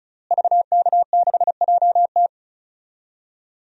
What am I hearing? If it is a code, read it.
Morse code, heard as a keyed tone, about 36 wpm, 700 Hz. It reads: VK6JT
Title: DXCC-100 Call Signs - 5 Characters - Repeated 2x - 35wpm